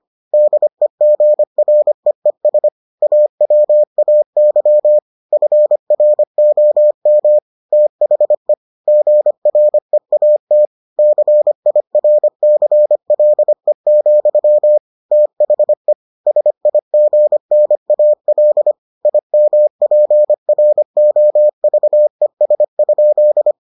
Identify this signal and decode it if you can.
Morse code, heard as a keyed tone, about 25 wpm, 600 Hz. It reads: DEGREES AWAY FROM THE GREAT CIRCLE, THE SIGNAL IMPROVES?